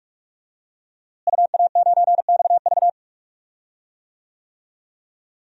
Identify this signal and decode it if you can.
Morse code, heard as a keyed tone, about 46 wpm, 700 Hz. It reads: UA9XV
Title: DXCC-100 Call Signs - 5 Characters - Repeated 3x - 45wpm